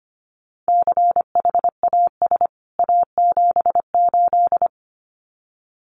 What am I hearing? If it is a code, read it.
Morse code, heard as a keyed tone, about 25 wpm, 700 Hz. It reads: CHAS A78
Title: CWT Exchange 25wpm